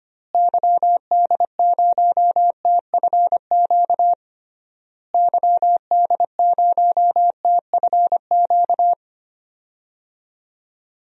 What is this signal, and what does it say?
Morse code, heard as a keyed tone, about 25 wpm, 700 Hz. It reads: YD0TFQ YD0TFQ
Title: DXCC-100 Call Signs - 6 Characters - Repeated 3x - 25wpm